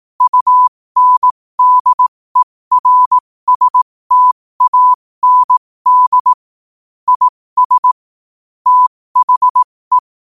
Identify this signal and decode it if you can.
Morse code, heard as a keyed tone, about 18 wpm, spaced as for 15 wpm, 1000 Hz. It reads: UNDERSTAND IS THE
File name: qotdfiles / 15wpm